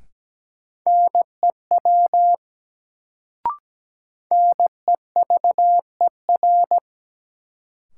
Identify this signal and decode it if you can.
Morse code, heard as a keyed tone, about 17 wpm, 700 Hz. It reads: NEW NEVER
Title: Sentences from Top 300 Words - New Word Drill - 17wpm